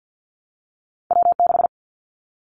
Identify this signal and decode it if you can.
Morse code, heard as a keyed tone, about 55 wpm, 700 Hz. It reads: W6
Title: Callsign Prefixes for Top 25 DXCC - Repeated 3x - 50wpm